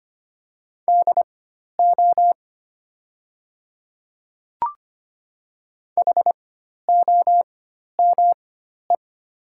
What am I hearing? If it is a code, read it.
Morse code, heard as a keyed tone, about 25 wpm, spaced as for 12 wpm, 700 Hz. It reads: DO HOME
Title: 4x Character Spacing - Top 200 Words - 25wpm